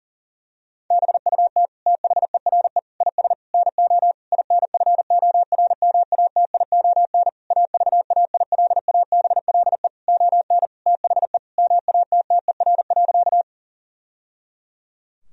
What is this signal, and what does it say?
Morse code, heard as a keyed tone, about 40 wpm, 700 Hz. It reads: BUT THERE IS NO INFORMATION AVAILABLE ON THE MATTER.
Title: Sentences from Top 1000 Words - Review Part 3 - 40wpm